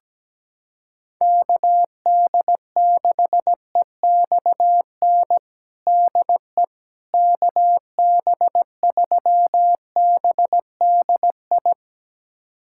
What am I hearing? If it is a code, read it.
Morse code, heard as a keyed tone, about 17 wpm, 700 Hz. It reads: KD6EXN DE KB3BDI
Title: USA Callsign DE Callsign 17wpm